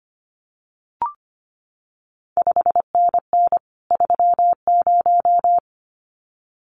Morse code, heard as a keyed tone, about 25 wpm, 700 Hz. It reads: 5NN 30
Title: CQ WW DX Contest Exchange Encore - 25wpm